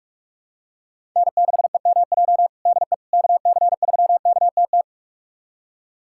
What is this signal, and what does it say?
Morse code, heard as a keyed tone, about 41 wpm, 690 Hz. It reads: N6EMJ DE KC3KTT
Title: USA Callsign DE Callsign 45wpm